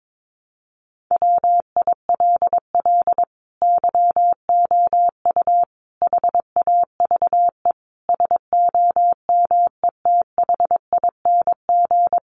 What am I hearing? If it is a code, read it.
Morse code, heard as a keyed tone, about 22 wpm, 700 Hz. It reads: WILL YOU HAVE SOMETHING
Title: Sentences from Top 600 Words 22wpm